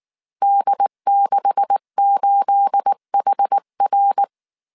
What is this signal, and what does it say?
Morse code, heard as a keyed tone, about 19 wpm, 800 Hz. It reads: D68HR